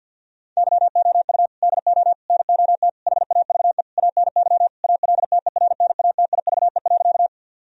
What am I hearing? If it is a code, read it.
Morse code, heard as a keyed tone, about 50 wpm, 700 Hz. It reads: YOU DO NOT HAVE ANY ALTERNATIVE.
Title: Sentences from Top 1000 Words - Review Part 4 - 50wpm